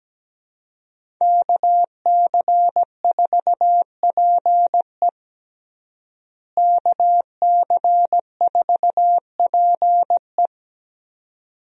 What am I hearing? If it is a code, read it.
Morse code, heard as a keyed tone, about 17 wpm, 700 Hz. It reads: KC4PE KC4PE
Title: USA Call Signs - 5 Characters - Repeated 2x - 17wpm